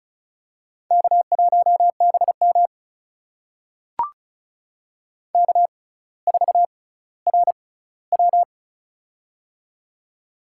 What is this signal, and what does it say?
Morse code, heard as a keyed tone, about 36 wpm, spaced as for 18 wpm, 700 Hz. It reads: K1BM K4RW
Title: USA Call Signs - 4 Characters - 6x Character Spacing - 35wpm